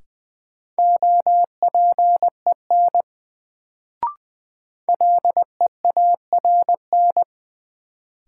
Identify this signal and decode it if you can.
Morse code, heard as a keyed tone, about 20 wpm, 700 Hz. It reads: OPEN LEARN
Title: Sentences from Top 300 Words - New Word Drill - 20wpm